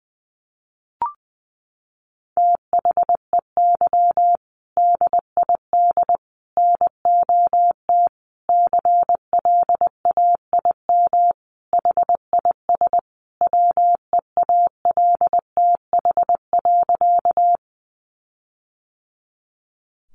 Morse code, heard as a keyed tone, about 20 wpm, 700 Hz. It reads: THEY DID NOT CLAIM HIS WEALTH.
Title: Sentences from Top 1500 Words 20wpm